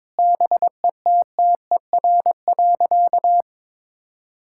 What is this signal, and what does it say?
Morse code, heard as a keyed tone, about 22 wpm, 700 Hz. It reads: BETTER.
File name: practice / 22wpm / Sentences with 2-Word Phrases - 22wpm